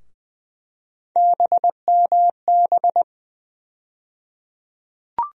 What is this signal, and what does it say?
Morse code, heard as a keyed tone, about 20 wpm, 700 Hz. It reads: BMB
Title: USA Call Sign Postfixes - 3 Letters 20wpm